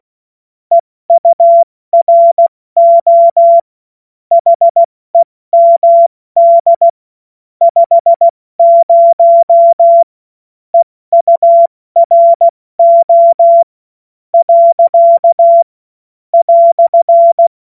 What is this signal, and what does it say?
Morse code, heard as a keyed tone, about 16 wpm, spaced as for 13 wpm, 680 Hz. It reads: EURO HEMD 50 EURO."
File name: Witz-30-September-2025-70-F80